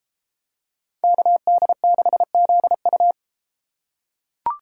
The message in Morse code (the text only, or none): KD6ZU